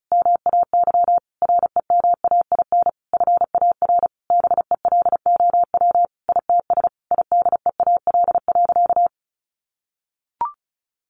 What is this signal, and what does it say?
Morse code, heard as a keyed tone, about 34 wpm, 700 Hz. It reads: MAY REMAIN FAR BELOW ITS IDEAL.
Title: Sentences from Top 1400 Words 35wpm